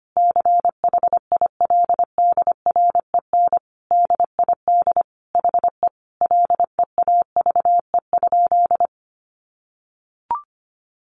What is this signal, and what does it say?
Morse code, heard as a keyed tone, about 25 wpm, 700 Hz. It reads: CHILDREN DID HE LEAVE?
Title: Sentences from Top 400 Words 25wpm